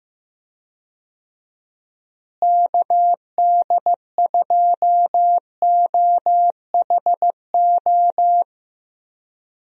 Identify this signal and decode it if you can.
Morse code, heard as a keyed tone, about 15 wpm, 700 Hz. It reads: KD2OHO
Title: USA Call Signs - 6 Characters - Repeated 2x - 15wpm